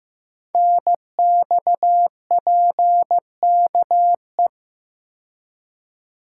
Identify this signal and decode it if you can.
Morse code, heard as a keyed tone, about 15 wpm, 700 Hz. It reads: NXPKE
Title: Code Groups - 5 Letters - 15wpm